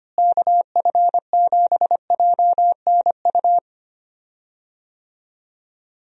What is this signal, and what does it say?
Morse code, heard as a keyed tone, about 25 wpm, 700 Hz. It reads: KF7JNU
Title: USA Call Signs - 6 Characters - Repeated 2x - 25wpm